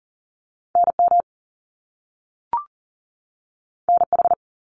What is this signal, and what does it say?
Morse code, heard as a keyed tone, about 41 wpm, 700 Hz. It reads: NM NH
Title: US States 40wpm